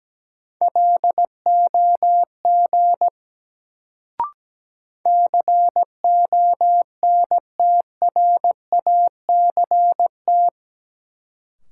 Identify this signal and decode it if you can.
Morse code, heard as a keyed tone, about 17 wpm, 700 Hz. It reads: LOG CONTRACT
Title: Sentences from Top 1400 Words - New Word Drill Alt - 17wpm